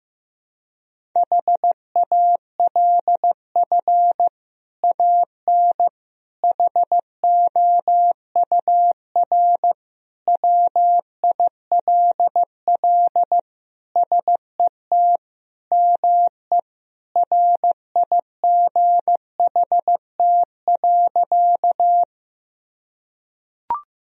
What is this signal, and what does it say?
Morse code, heard as a keyed tone, about 15 wpm, 700 Hz. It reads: HALF AN HOUR WILL SET ME RIGHT.
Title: Sentences from Top 1000 Words - Review Part 1 - 15wpm